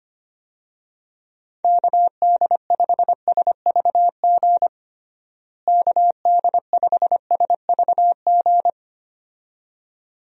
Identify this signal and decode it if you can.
Morse code, heard as a keyed tone, about 25 wpm, 700 Hz. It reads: KD5SVG KD5SVG
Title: USA Call Signs - 6 Characters - Repeated 2x - 25wpm